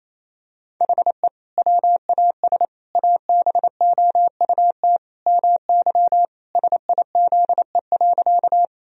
Morse code, heard as a keyed tone, about 28 wpm, 700 Hz. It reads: HE WAS ABOUT MY SIZE.
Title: Sentences from Top 1000 Words - Review Part 2 - 28wpm